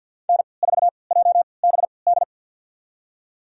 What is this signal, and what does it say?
Morse code, heard as a keyed tone, about 46 wpm, spaced as for 27 wpm, 700 Hz. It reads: N4JBD
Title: USA Call Signs - 5 Characters - 3x Character Spacing - 50wpm